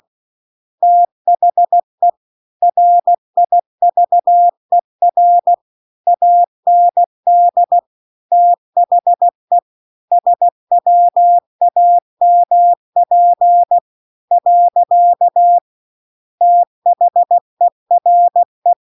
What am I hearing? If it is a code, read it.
Morse code, heard as a keyed tone, about 16 wpm, 700 Hz. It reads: THE RIVER AND THE SWAMP . THERE
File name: 16WPM